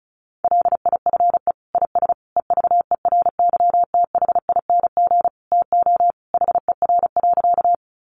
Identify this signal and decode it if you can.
Morse code, heard as a keyed tone, about 34 wpm, 700 Hz. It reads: LIFE IS EVERYTHING TO HER.
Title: Sentences from Top 700 Words 35wpm